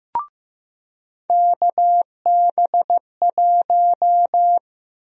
Morse code, heard as a keyed tone, about 16 wpm, 700 Hz. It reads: KB1
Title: USA Call Sign Prefixes - 15wpm